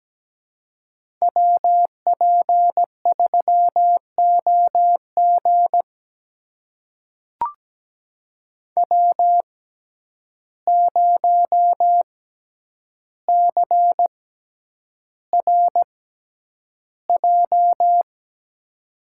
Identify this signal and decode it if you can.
Morse code, heard as a keyed tone, about 17 wpm, spaced as for 9 wpm, 700 Hz. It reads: WP3OG W0CRJ
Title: USA Call Signs - 5 Characters - 6x Character Spacing - 17wpm